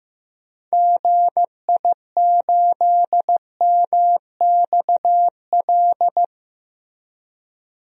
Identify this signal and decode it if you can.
Morse code, heard as a keyed tone, about 15 wpm, 700 Hz. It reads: GI8MXL
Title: DXCC-100 Call Signs - 6 Characters - 15wpm